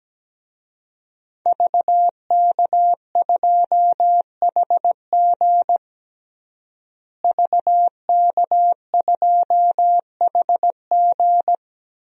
Morse code, heard as a keyed tone, about 17 wpm, 700 Hz. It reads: VK2HG VK2HG
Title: DXCC-100 Call Signs - 5 Characters - Repeated 3x - 17wpm